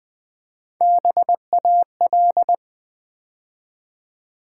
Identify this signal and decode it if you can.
Morse code, heard as a keyed tone, about 20 wpm, 700 Hz. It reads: BAL
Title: Build A Word - 3 Letters At A Time - 6-Letter Words - 20wpm